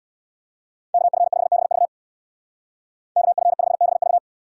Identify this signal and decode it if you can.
Morse code, heard as a keyed tone, about 63 wpm, 700 Hz. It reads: KF5BV KF5BV